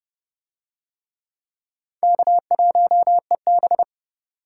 Morse code, heard as a keyed tone, about 30 wpm, 700 Hz. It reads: K1EB